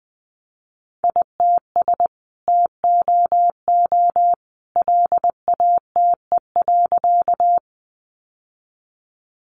Morse code, heard as a keyed tone, about 20 wpm, 700 Hz. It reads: ITS TOO LATE.
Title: Sentences from Top 600 Words 20wpm